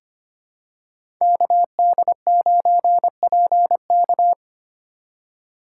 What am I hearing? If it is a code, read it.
Morse code, heard as a keyed tone, about 25 wpm, 700 Hz. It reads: KD9PK
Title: USA Call Signs - 5 Characters - 4x Character Spacing - 25wpm